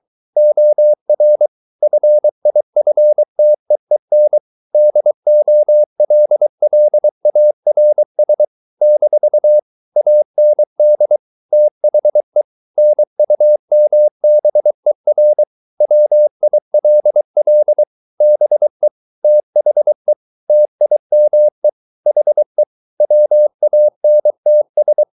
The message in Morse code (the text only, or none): OR FIFTEEN DOLLARS - AND THE NUMBER WILL BE THE TIME HE WANTS